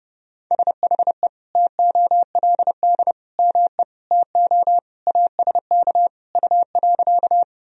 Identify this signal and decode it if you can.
Morse code, heard as a keyed tone, about 32 wpm, 700 Hz. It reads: SHE TOLD ME TO ASK U.
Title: Sentences from Top 400 Words and Abbreviations 30wpm